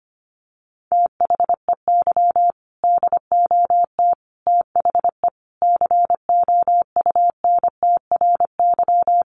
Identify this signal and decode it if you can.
Morse code, heard as a keyed tone, about 25 wpm, 700 Hz. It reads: THEY DOT THE COUNTRY
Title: Sentences from Top 1500 Words 25wpm